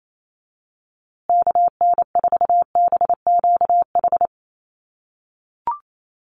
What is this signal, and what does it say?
Morse code, heard as a keyed tone, about 28 wpm, 700 Hz. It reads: KN4BQH
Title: USA Call Signs Repeated 2x 28wpm